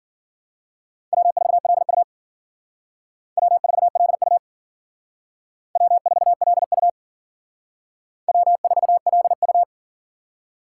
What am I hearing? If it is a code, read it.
Morse code, heard as a keyed tone, about 49 wpm, 700 Hz. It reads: W4LU W4LU W4LU W4LU
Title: Speed Racing - USA 1x2 Call Signs 40wpm